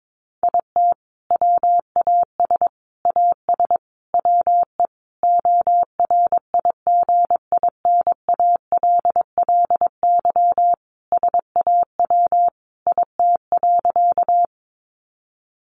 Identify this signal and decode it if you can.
Morse code, heard as a keyed tone, about 22 wpm, 700 Hz. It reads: IT WAS AS WE ORIGINALLY SAW IT.
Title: Sentences from Top 1500 Words 22wpm